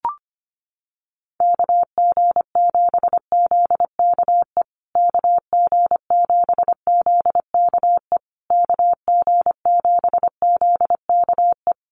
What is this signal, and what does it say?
Morse code, heard as a keyed tone, about 25 wpm, 700 Hz. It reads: KG7ZKE KG7ZKE KG7ZKE
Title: USA Call Signs Repeated 3x 25wpm